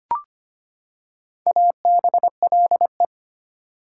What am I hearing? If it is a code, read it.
Morse code, heard as a keyed tone, about 25 wpm, 700 Hz. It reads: ABLE
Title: Prefixes and Suffixes 25wpm